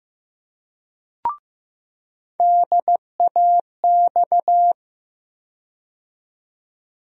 Morse code, heard as a keyed tone, about 15 wpm, 700 Hz. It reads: DAX
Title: USA Call Sign Postfixes - 3 Letters 15wpm